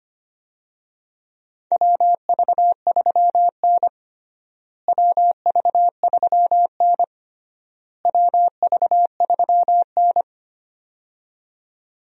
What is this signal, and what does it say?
Morse code, heard as a keyed tone, about 25 wpm, 700 Hz. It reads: WV3N WV3N WV3N
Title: USA Call Signs - 4 Characters - Repeated 3x - 25wpm